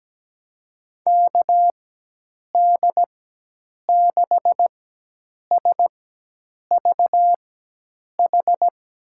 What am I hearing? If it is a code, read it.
Morse code, heard as a keyed tone, about 17 wpm, spaced as for 8 wpm, 700 Hz. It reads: KD6SVH
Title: USA Call Signs - 4x Character Spacing - 17wpm